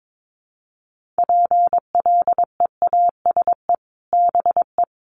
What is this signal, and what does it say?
Morse code, heard as a keyed tone, about 22 wpm, 700 Hz. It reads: PLEASE BE
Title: Sentences from Top 1500 Words 22wpm